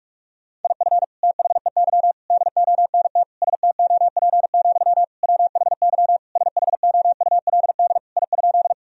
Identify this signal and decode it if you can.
Morse code, heard as a keyed tone, about 46 wpm, 700 Hz. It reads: IF THEY DONT STOP, WHY SHOULD I?
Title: Sentences from Top 1000 Words - Review Part 1 - 45wpm